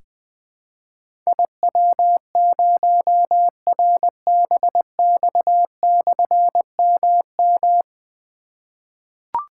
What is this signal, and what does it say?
Morse code, heard as a keyed tone, about 20 wpm, 700 Hz. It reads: IW0RBX/MM
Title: Difficult Call Signs 20wpm